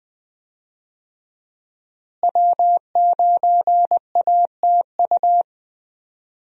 Morse code, heard as a keyed tone, about 20 wpm, 700 Hz. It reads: W9ATU
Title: USA Call Signs - 5 Characters - Repeated 3x - 20wpm